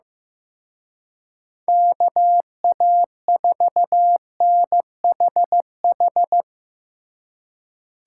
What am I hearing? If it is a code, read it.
Morse code, heard as a keyed tone, about 17 wpm, 700 Hz. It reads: KA4NHH
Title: Speed Racing - USA 2x3 Call Signs 15wpm